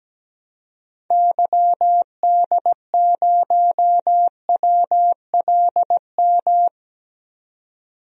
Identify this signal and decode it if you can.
Morse code, heard as a keyed tone, about 17 wpm, 700 Hz. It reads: YD0WLM
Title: DXCC-100 Call Signs - 6 Characters - 17wpm